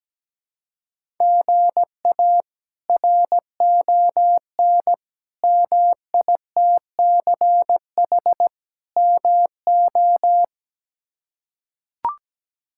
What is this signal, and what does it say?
Morse code, heard as a keyed tone, about 17 wpm, 700 Hz. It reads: GA RON MITCH MO
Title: K1USN SST QSO Practice - 17wpm